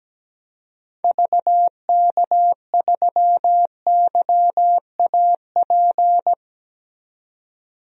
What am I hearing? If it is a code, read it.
Morse code, heard as a keyed tone, about 17 wpm, 700 Hz. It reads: VK3YAP
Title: DXCC-100 Call Signs - 6 Characters - 17wpm